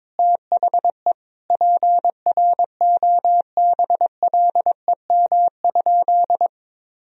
Morse code, heard as a keyed tone, about 21 wpm, 700 Hz. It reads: THE PROBLEM?
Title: Sentences from Top 1000 Words - Review Part 4 - 22wpm